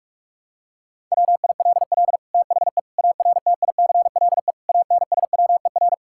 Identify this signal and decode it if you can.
Morse code, heard as a keyed tone, about 46 wpm, 700 Hz. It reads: WILL THE ARTICLE ANSWER
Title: Sentences from Top 1000 Words - Review Part 5 - 45wpm